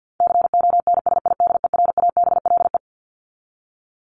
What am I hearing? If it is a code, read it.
Morse code, heard as a keyed tone, about 50 wpm, 700 Hz. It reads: CONSIDERABLE
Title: LY Suffix - Top 100 Words with Base Words 50wpm